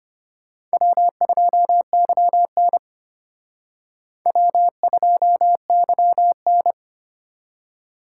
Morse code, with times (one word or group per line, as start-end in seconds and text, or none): 0.7-2.8 W2YN
4.3-6.7 W2YN